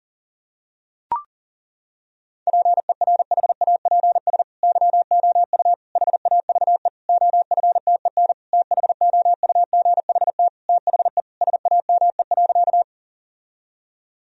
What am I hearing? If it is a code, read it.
Morse code, heard as a keyed tone, about 40 wpm, 700 Hz. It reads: PERHAPS YOU HAVE OFTEN THOUGHT THE SAME.
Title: Sentences from Top 1000 Words - Review Part 1 - 40wpm